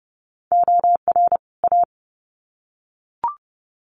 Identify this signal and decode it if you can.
Morse code, heard as a keyed tone, about 30 wpm, 700 Hz. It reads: OR A
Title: Sets of 2 Words 30wpm